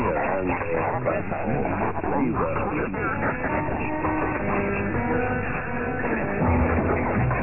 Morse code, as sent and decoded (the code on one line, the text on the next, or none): none